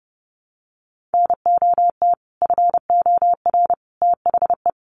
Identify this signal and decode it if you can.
Morse code, heard as a keyed tone, about 30 wpm, 700 Hz. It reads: NOT FOR THE
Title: Sentences from Top 600 Words 30wpm